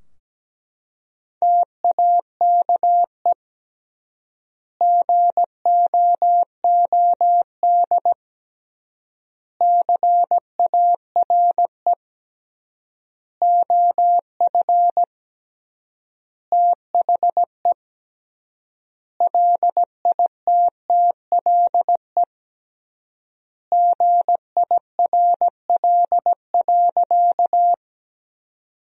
Morse code, heard as a keyed tone, about 17 wpm, 700 Hz. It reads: TAKE GOOD CARE OF THE LITTLE GIRL.
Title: 3x Word-Spacing - Sentences from Top 400 Words - 17wpm